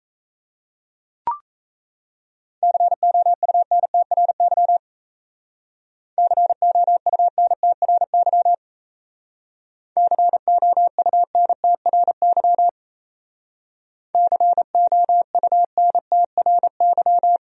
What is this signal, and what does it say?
Morse code, heard as a keyed tone, about 35 wpm, 700 Hz. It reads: COUNTRY COUNTRY COUNTRY COUNTRY